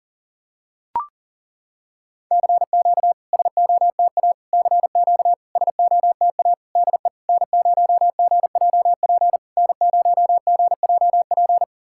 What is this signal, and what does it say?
Morse code, heard as a keyed tone, about 40 wpm, 700 Hz. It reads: CQ SOTA CQ SOTA DE N0GJP N0GJP
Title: SOTA QSO Practice 40wpm